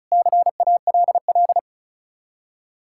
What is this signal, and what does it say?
Morse code, heard as a keyed tone, about 43 wpm, 700 Hz. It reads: CALL
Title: Speed-Racing Common QSO Phrases Encore - 35wpm